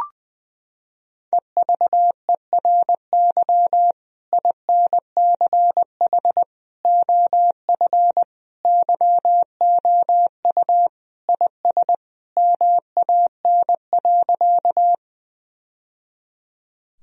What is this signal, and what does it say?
Morse code, heard as a keyed tone, about 20 wpm, 700 Hz. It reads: EVERY INCH OF YOU IS MAN.
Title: Sentences from Top 1000 Words - Review Part 3 - 20wpm